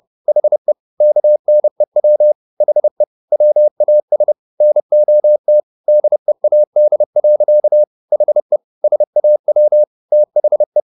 Morse code, heard as a keyed tone, about 30 wpm, 600 Hz. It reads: HE KNEW HE WAS NOT DEAD. HE SAW THE